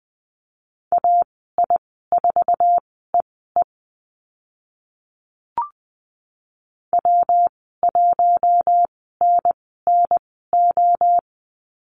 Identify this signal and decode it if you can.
Morse code, heard as a keyed tone, about 20 wpm, spaced as for 14 wpm, 700 Hz. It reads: AI4EE W1NNO
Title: 122 Lesson 20g - Number 3 - Call Signs 20wpm